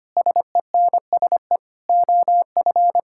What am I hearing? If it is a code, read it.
Morse code, heard as a keyed tone, about 25 wpm, 700 Hz. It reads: SENSE OF
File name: practice / 25wpm / Difficult Sentences with Hints - 25wpm